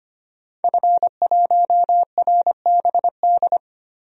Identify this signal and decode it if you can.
Morse code, heard as a keyed tone, about 25 wpm, 700 Hz. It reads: F1RBD
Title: DXCC-100 Call Signs - 5 Characters - Repeated 2x - 25wpm